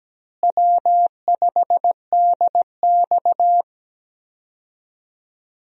Morse code, heard as a keyed tone, about 17 wpm, 700 Hz. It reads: W5DX
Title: USA Call Signs - 4 Characters - Repeated 2x - 17wpm